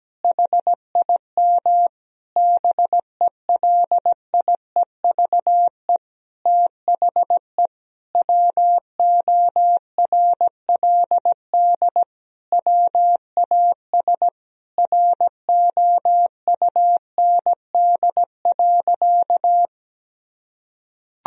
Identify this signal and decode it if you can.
Morse code, heard as a keyed tone, about 17 wpm, 700 Hz. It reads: HIM BELIEVE THE WORLD WAS ROUND.
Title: Sentences from Top 1000 Words - Review Part 1 - 17wpm